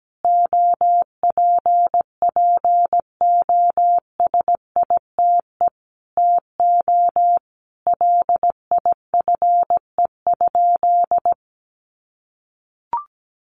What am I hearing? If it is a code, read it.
Morse code, heard as a keyed tone, about 17 wpm, 700 Hz. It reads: OPPOSITE TO LIFE?
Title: Sentences from Top 1300 Words 17wpm